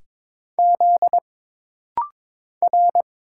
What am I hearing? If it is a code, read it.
Morse code, heard as a keyed tone, about 22 wpm, 700 Hz. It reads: Z R
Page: This rapid-fire timing is geared to push your ICR (Instant Character Recognition) skills.